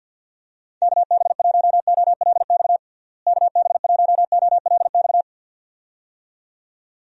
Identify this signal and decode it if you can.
Morse code, heard as a keyed tone, about 45 wpm, 700 Hz. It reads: KB1OLX KB1OLX
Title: USA Call Signs - 6 Characters - Repeated 2x - 50wpm